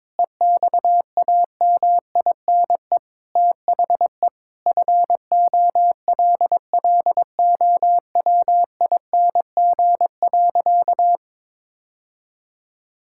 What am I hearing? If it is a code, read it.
Morse code, heard as a keyed tone, about 22 wpm, 700 Hz. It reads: EXAMINE THE FOLLOWING.
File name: Sentences with 9-Word Phrases - 22wpm